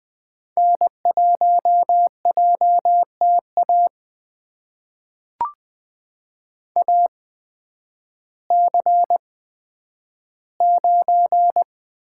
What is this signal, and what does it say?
Morse code, heard as a keyed tone, about 20 wpm, spaced as for 11 wpm, 700 Hz. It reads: N1JTA AC9
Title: USA Call Signs - 5 Characters - 8x Character Spacing - 20wpm